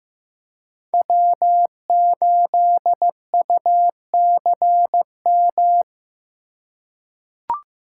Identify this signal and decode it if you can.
Morse code, heard as a keyed tone, about 15 wpm, 700 Hz. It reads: W8UCM
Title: USA Call Signs - 8x Character Spacing - 15wpm